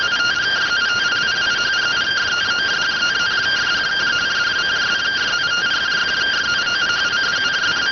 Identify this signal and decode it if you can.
Two-tone teleprinter signal, about 170 Hz shift.